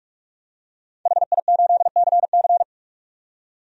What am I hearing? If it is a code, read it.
Morse code, heard as a keyed tone, about 44 wpm, 690 Hz. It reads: HI8CC
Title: DXCC-100 Call Signs - 5 Characters - 45wpm